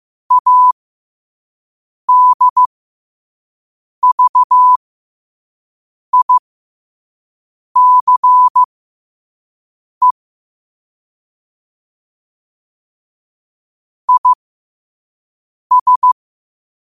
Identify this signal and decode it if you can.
Morse code, heard as a keyed tone, about 15 wpm, spaced as for 5 wpm, 1000 Hz. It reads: ADVICE IS
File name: qotdfiles / 5wpm